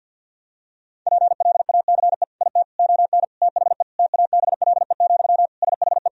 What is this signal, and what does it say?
Morse code, heard as a keyed tone, about 50 wpm, 700 Hz. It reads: PLACE IT ON THE TABLE, SHE
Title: Sentences from Top 1000 Words - Review Part 2 - 50wpm